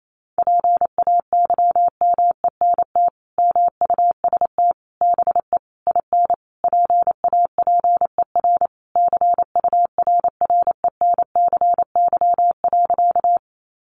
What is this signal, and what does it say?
Morse code, heard as a keyed tone, about 28 wpm, 700 Hz. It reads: PAYMENT MUST BE IN PAPER CURRENCY.
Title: Sentences from Top 1500 Words 28wpm